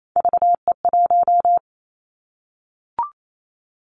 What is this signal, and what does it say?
Morse code, heard as a keyed tone, about 28 wpm, 700 Hz. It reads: VE1
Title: Callsign Prefixes for Top 100 DXCC - 28wpm